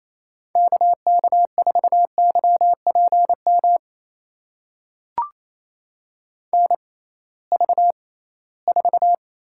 Morse code, heard as a keyed tone, about 28 wpm, spaced as for 18 wpm, 700 Hz. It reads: KK4YPM NV4
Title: USA Call Signs - 6x Character Spacing - 28wpm